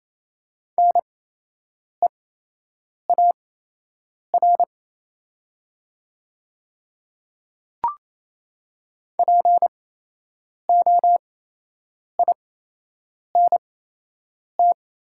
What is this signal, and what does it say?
Morse code, heard as a keyed tone, about 27 wpm, spaced as for 8 wpm, 700 Hz. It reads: NEAR POINT
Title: 8x Character Spacing - Top 300 Words - 28wpm